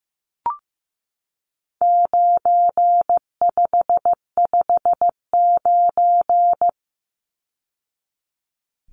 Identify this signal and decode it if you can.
Morse code, heard as a keyed tone, about 15 wpm, 700 Hz. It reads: 9559
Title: Numbers - 4-Digit 15wpm